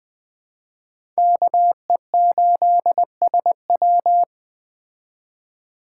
Morse code, heard as a keyed tone, about 21 wpm, 700 Hz. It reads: KE8SW
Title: Speed Racing - USA Call Signs 17wpm